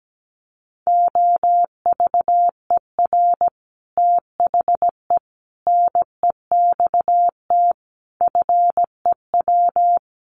Morse code, heard as a keyed tone, about 17 wpm, 700 Hz. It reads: OVER THE NEXT FEW
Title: Sets of 5 Words Encore 17wpm